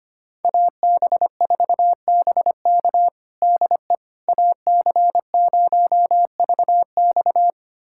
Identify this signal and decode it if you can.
Morse code, heard as a keyed tone, about 25 wpm, 700 Hz. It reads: AB4BK DE AC0VX
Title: USA Callsign DE Callsign - 5 Characters - 25wpm